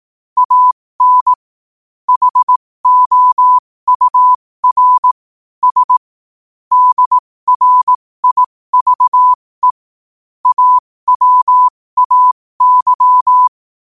Morse code, heard as a keyed tone, about 18 wpm, spaced as for 15 wpm, 1000 Hz. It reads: AN HOUR S DRIVE AWAY
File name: qotdfiles / 15wpm